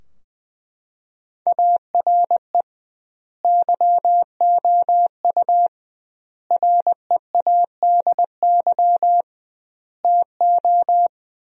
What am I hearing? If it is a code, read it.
Morse code, heard as a keyed tone, about 20 wpm, 700 Hz. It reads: ARE YOU READY TO
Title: 2x Word-Spacing - Sentences from Top 400 Words - 20wpm